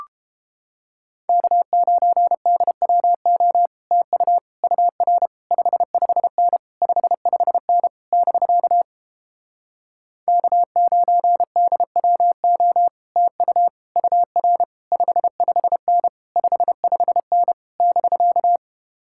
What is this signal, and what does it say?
Morse code, heard as a keyed tone, about 29 wpm, 700 Hz. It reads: K9DWO TU UR 55N 55N <BK> K9DWO TU UR 55N 55N <BK>